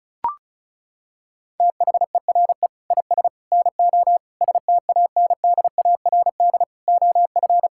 Morse code, heard as a keyed tone, about 34 wpm, 700 Hz. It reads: THERE IS NO STANDARD OF
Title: Sentences from Top 1600 Words 35wpm